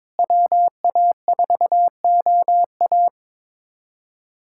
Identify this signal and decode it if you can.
Morse code, heard as a keyed tone, about 21 wpm, 700 Hz. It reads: WA4OA
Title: USA Callsign DE Callsign - 5 Characters - 22wpm